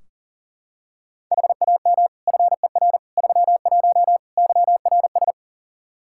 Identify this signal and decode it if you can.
Morse code, heard as a keyed tone, about 40 wpm, 700 Hz. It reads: HAM FER 31 YRS
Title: K1USN SST QSO Practice 40wpm